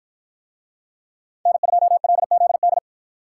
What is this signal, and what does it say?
Morse code, heard as a keyed tone, about 53 wpm, 700 Hz. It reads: N2LZD